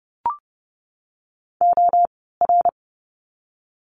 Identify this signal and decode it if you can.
Morse code, heard as a keyed tone, about 30 wpm, spaced as for 17 wpm, 700 Hz. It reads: OR
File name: course / 30:10wpm / 115 Lesson 19f - Letter w - Words 30wpm